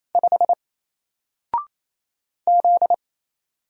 Morse code, heard as a keyed tone, about 28 wpm, 700 Hz. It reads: 5 Z
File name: Single-Letter-Number-28wpm